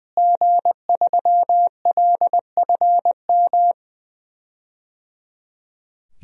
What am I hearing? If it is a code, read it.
Morse code, heard as a keyed tone, about 21 wpm, 700 Hz. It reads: G3LFM
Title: DXCC-100 Call Signs - 5 Characters - Repeated 3x - 20wpm